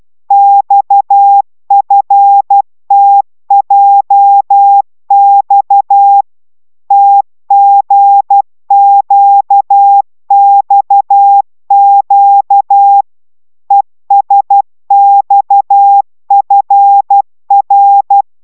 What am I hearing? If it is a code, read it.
Morse code, heard as a keyed tone, about 12 wpm, 800 Hz. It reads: XFTJX TGQXQ ESXFR